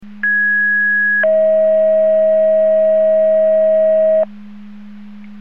Two-tone page, heard as a one-second tone, then a three-second tone, about 1650 then 640 Hz. It is heard from 0.2 to 4.2 s.
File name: Portage / 00081 11-01 Portage 154.385 Paging